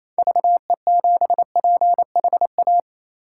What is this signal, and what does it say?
Morse code, heard as a keyed tone, about 28 wpm, 700 Hz. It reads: VE7PHA
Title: K1USN SST QSO Practice - 28wpm